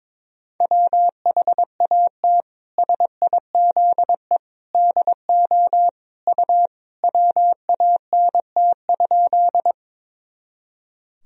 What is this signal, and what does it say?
Morse code, heard as a keyed tone, about 22 wpm, 700 Hz. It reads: WHAT SIZE DO U WANT?
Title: Sentences from Top 400 Words and Abbreviations 22wpm